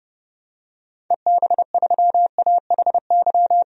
Morse code, heard as a keyed tone, about 30 wpm, 700 Hz. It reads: EB3AHY
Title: DXCC-100 Call Signs - 6 Characters - 30wpm